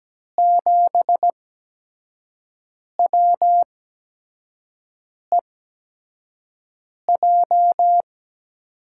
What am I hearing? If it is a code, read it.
Morse code, heard as a keyed tone, about 17 wpm, spaced as for 5 wpm, 700 Hz. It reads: 7WEJ
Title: USA Call Signs - 8x Character Spacing - 17wpm